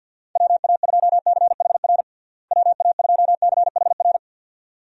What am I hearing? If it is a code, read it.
Morse code, heard as a keyed tone, about 50 wpm, 700 Hz. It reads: WA2CHR WA2CHR
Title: USA Call Signs - 6 Characters - Repeated 2x - 50wpm